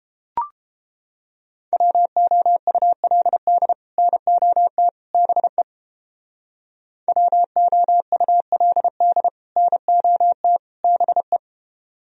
Morse code, heard as a keyed tone, about 31 wpm, 700 Hz. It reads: WOULD NOT BE WOULD NOT BE